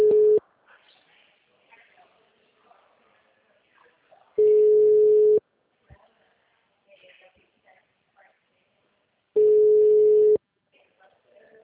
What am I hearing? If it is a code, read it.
Ringback tone (Europe, 425 Hz), with speech.
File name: out- -2011- - - .148